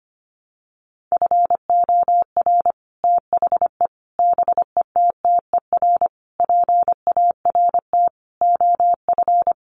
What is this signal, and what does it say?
Morse code, heard as a keyed tone, about 25 wpm, 700 Hz. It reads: FOR THE BETTER PART OF
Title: Sets of 5 Words Encore2 25wpm